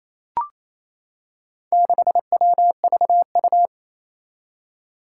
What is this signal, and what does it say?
Morse code, heard as a keyed tone, about 28 wpm, 700 Hz. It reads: 6WVU
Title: Four Challenging Letter-Number Combinations 28wpm